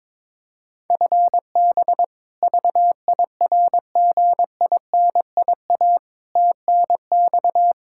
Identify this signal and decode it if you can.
Morse code, heard as a keyed tone, about 22 wpm, 700 Hz. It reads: FB VIRGINIA TNX
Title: K1USN SST QSO Practice 22wpm